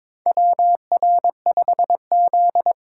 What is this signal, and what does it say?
Morse code, heard as a keyed tone, about 22 wpm, 700 Hz. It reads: WR5Z
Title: USA Call Signs - 4 Characters - 3x Character Spacing - 22wpm